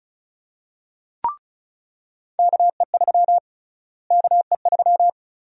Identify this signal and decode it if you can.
Morse code, heard as a keyed tone, about 36 wpm, 700 Hz. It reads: KE3 KE3
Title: USA Call Sign Prefixes - Repeated 2x - 35wpm